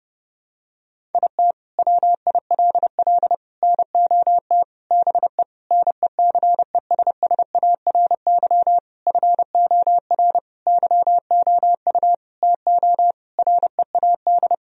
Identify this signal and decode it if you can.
Morse code, heard as a keyed tone, about 32 wpm, 700 Hz. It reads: IT WILL NOT BE NECESSARY FOR YOU TO READ
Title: Sentences from Top 1000 Words - Review Part 1 - 30wpm